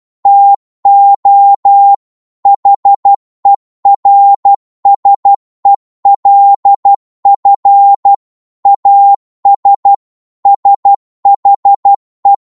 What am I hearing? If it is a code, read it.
Morse code, heard as a keyed tone, about 12 wpm, 800 Hz. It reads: TO HERSELF AS SHE
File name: Morse / psg-12wpm